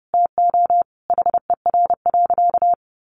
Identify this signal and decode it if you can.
Morse code, heard as a keyed tone, about 30 wpm, 700 Hz. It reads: TO HER.
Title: Sentences from Top 1100 Words 30wpm